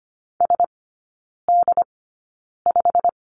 Morse code, heard as a keyed tone, about 25 wpm, spaced as for 9 wpm, 690 Hz. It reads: SD5
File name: Lesson 15e - Number 5 - Three Characters 25wpm